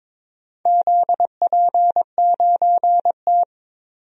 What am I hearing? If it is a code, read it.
Morse code, heard as a keyed tone, about 22 wpm, 690 Hz. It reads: ZP9T
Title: DXCC-100 Call Signs - 4 Characters - 22wpm